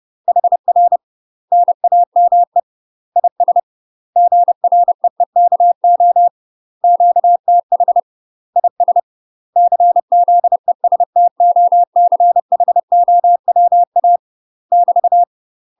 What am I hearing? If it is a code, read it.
Morse code, heard as a keyed tone, about 30 wpm, 700 Hz. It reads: HR NAME IS GREEKO QTH IS CZESTOCHOWA <BT>
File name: UK MASTER QSOs TRAINING/30WPM